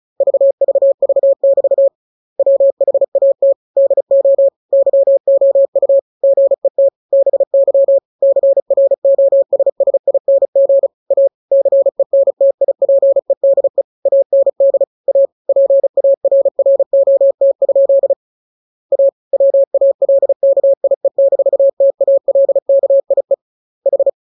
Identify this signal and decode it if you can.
Morse code, heard as a keyed tone, about 34 wpm, 550 Hz. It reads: VVV<BT> WHAT DO YOU GET BY CROSSING A CENTIPEDE AND A PARROT? A WALKIE-TALKIE H